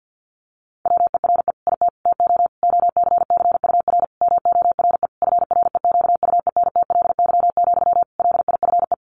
Morse code, heard as a keyed tone, about 50 wpm, 700 Hz. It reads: WERE IT TO OCCUR MORE FREQUENTLY, LIFE
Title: Sentences from Top 1300 Words 50wpm